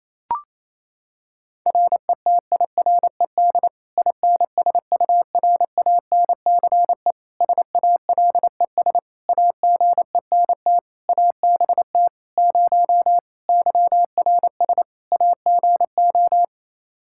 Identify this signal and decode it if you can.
Morse code, heard as a keyed tone, about 28 wpm, 700 Hz. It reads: RETIRED INSURANCE SALES AGENT ABT 0 YRS AGO
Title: Practice QSO - Exchange 3 - More Info - 28wpm